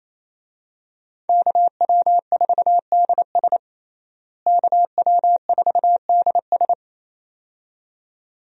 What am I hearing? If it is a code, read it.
Morse code, heard as a keyed tone, about 28 wpm, 700 Hz. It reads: KW4DS KW4DS
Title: USA Call Signs - 5 Characters - Repeated 2x - 28wpm